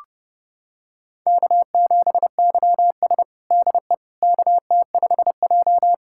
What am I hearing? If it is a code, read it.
Morse code, heard as a keyed tone, about 30 wpm, 700 Hz. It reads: K7YS DE KT5J
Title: USA Callsign DE Callsign - 4 Characters - 30wpm